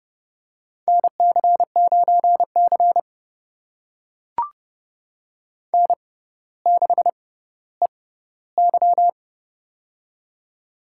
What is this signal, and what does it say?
Morse code, heard as a keyed tone, about 30 wpm, spaced as for 15 wpm, 700 Hz. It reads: NC9C N6EY
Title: USA Call Signs - 6x Character Spacing - 30wpm